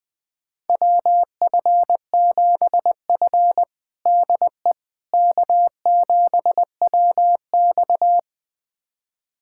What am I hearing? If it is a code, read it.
Morse code, heard as a keyed tone, about 20 wpm, 700 Hz. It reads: WF7F DE K7WX
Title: USA Callsign DE Callsign - 4 Characters - 20wpm